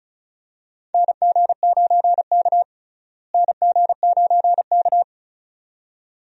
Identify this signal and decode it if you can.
Morse code, heard as a keyed tone, about 36 wpm, 700 Hz. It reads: NG9K NG9K
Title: USA Call Signs - 4 Characters - Repeated 2x - 35wpm